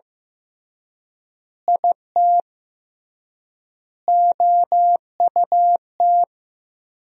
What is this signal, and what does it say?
Morse code, heard as a keyed tone, about 15 wpm, 700 Hz. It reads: IT OUT
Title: 3x Word-Spacing - Sentences from Top 200 Words - 15wpm